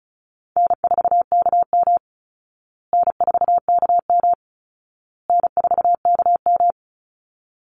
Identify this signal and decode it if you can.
Morse code, heard as a keyed tone, about 34 wpm, 700 Hz. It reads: N4KM N4KM N4KM
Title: Call Signs Repeated 2x - 35wpm